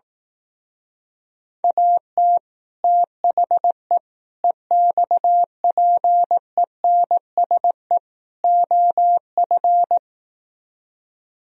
Morse code, heard as a keyed tone, about 18 wpm, 700 Hz. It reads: AT THE EXPENSE OF
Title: Speed-Racing - Sets of 4 Words 15wpm